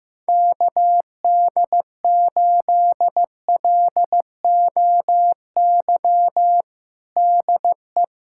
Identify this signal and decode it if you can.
Morse code, heard as a keyed tone, about 15 wpm, 700 Hz. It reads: KD8LOY DE
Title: USA Callsign DE Callsign - 6 Characters - 15wpm